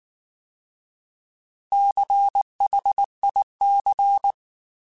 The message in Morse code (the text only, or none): CHIC